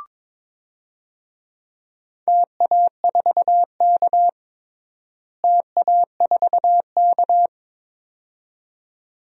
Morse code, heard as a keyed tone, about 22 wpm, 700 Hz. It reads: TA4K TA4K
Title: DXCC-100 Call Signs - 4 Characters - Repeated 2x - 22wpm